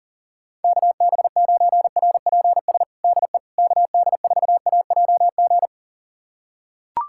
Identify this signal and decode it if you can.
Morse code, heard as a keyed tone, about 40 wpm, 700 Hz. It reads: KB9RPS DE KD4AJG
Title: USA Callsign DE Callsign - 6 Characters - 40wpm